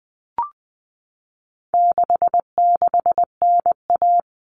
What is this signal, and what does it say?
Morse code, heard as a keyed tone, about 20 wpm, 700 Hz. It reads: 66NA
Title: All Cut Numbers - 4-Digit 20wpm